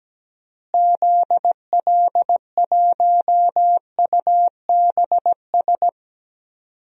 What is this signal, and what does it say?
Morse code, heard as a keyed tone, about 17 wpm, 700 Hz. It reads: ZL1UBS
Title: DXCC-100 Call Signs - 6 Characters - Repeated 3x - 17wpm